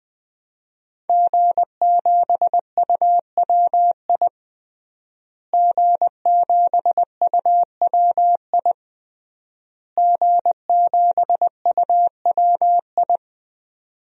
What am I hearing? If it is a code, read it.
Morse code, heard as a keyed tone, about 21 wpm, 700 Hz. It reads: G7UWI G7UWI G7UWI
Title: DXCC-100 Call Signs - 5 Characters - Repeated 3x - 20wpm